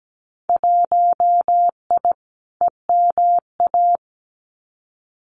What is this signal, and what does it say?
Morse code, heard as a keyed tone, about 17 wpm, 700 Hz. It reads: 1I EMA
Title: Winter Field Day Exchanges Encore 17wpm